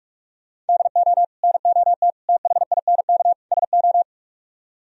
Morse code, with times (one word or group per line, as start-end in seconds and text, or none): 0.7-1.2 DO
1.4-2.1 NOT
2.3-3.3 THINK
3.5-4.0 SO